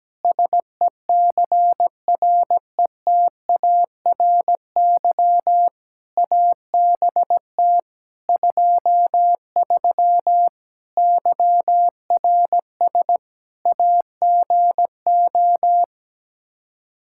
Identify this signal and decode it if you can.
Morse code, heard as a keyed tone, about 17 wpm, 700 Hz. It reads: SECRETARY ABT 23 YRS AGO
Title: K1USN SST QSO Practice 17wpm